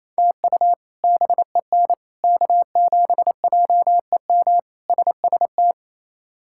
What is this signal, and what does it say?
Morse code, heard as a keyed tone, about 27 wpm, 700 Hz. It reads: TU BEN K7JEM SST
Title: K1USN SST QSO Practice - 28wpm